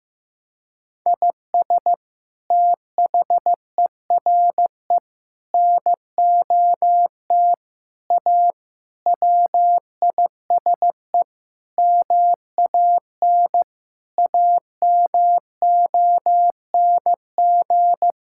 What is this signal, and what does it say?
Morse code, heard as a keyed tone, about 16 wpm, 700 Hz. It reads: IS THERE NOT A WISE MAN AMONG
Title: Sentences from Top 1600 Words 15wpm